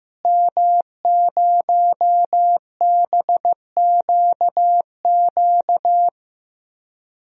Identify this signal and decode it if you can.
Morse code, heard as a keyed tone, about 15 wpm, 700 Hz. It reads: M0BQQ
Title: DXCC-100 Call Signs - 5 Characters - Repeated 2x - 15wpm